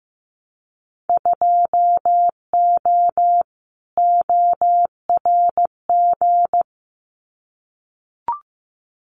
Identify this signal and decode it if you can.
Morse code, heard as a keyed tone, about 15 wpm, 700 Hz. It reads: 2O ORG
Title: Winter Field Day Exchanges 15wpm